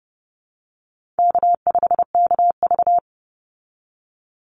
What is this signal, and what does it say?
Morse code, heard as a keyed tone, about 30 wpm, 700 Hz. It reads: K5KV
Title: Call Signs Repeated 3x - 30wpm